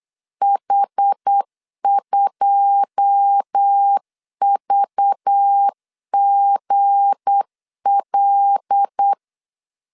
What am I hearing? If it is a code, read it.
Morse code, heard as a keyed tone, about 8 wpm, 800 Hz. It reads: H2VGL
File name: LEZIONI / 9 2 8wpm 3437